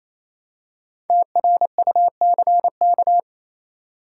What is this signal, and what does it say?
Morse code, heard as a keyed tone, about 28 wpm, 700 Hz. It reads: TRUCK